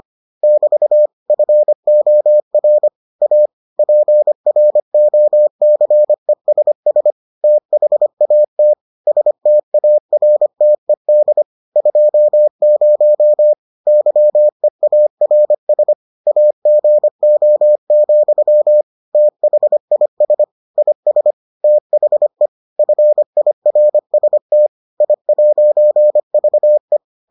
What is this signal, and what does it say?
Morse code, heard as a keyed tone, about 25 wpm, 600 Hz. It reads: <BT> FOR A PROCESS THAT STARTED 20 YEARS AGO, THIS IS THE FIRST I'VE